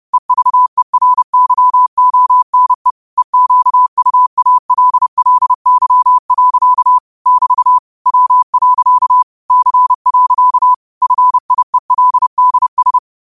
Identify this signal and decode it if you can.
Morse code, heard as a keyed tone, about 30 wpm, 1000 Hz. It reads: EVERYONE EQUALLY. <BT> W. C. FIELDS
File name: qotdfiles / 30wpm